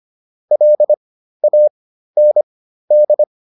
Morse code, heard as a keyed tone, about 25 wpm, spaced as for 13 wpm, 600 Hz. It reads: LAND
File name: PR404 13